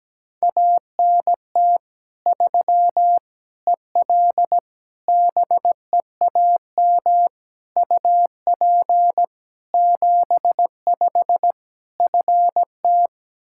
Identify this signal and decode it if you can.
Morse code, heard as a keyed tone, about 17 wpm, 700 Hz. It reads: ANT 3 EL BEAM UP 75 FT
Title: Practice QSO - Exchange 2 - Info - 17wpm